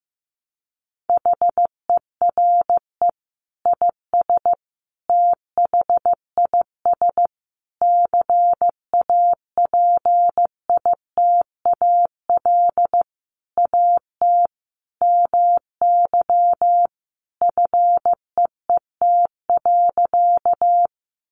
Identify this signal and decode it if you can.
Morse code, heard as a keyed tone, about 15 wpm, 700 Hz. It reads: HERE IS THIS CAPITAL AT MY FEET.
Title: Sentences from Top 1000 Words 15wpm